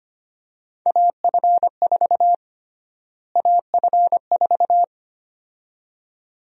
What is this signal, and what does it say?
Morse code, heard as a keyed tone, about 25 wpm, 700 Hz. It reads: AF4 AF4
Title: USA Call Sign Prefixes - Repeated 2x - 25wpm